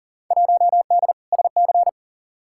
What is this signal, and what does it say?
Morse code, heard as a keyed tone, about 37 wpm, 700 Hz. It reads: 1D SC
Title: Field Day Exchanges - 40wpm